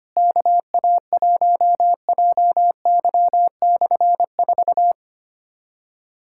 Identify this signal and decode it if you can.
Morse code, heard as a keyed tone, about 25 wpm, 700 Hz. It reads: KA1JY/4
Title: K1USN SST QSO Practice - 25wpm